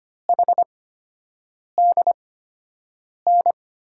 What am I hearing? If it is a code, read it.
Morse code, heard as a keyed tone, about 25 wpm, spaced as for 7 wpm, 700 Hz. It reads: HDN
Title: USA Call Signs - 8x Character Spacing - 25wpm